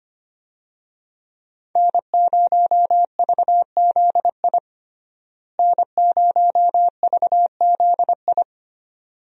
Morse code, heard as a keyed tone, about 25 wpm, 700 Hz. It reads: N0VZI N0VZI
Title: USA Call Signs - 5 Characters - Repeated 2x - 25wpm